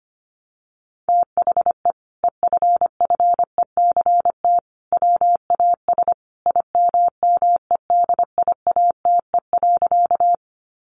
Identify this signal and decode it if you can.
Morse code, heard as a keyed tone, about 25 wpm, 700 Hz. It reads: THE EFFECT WAS IMMEDIATE.
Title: Sentences from Top 1200 Words 25wpm